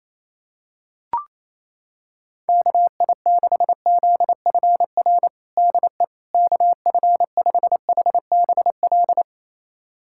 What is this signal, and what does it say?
Morse code, heard as a keyed tone, about 28 wpm, 700 Hz. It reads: KI6ZFR DE KF5HBL
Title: USA Callsign DE Callsign - 6 Characters - 28wpm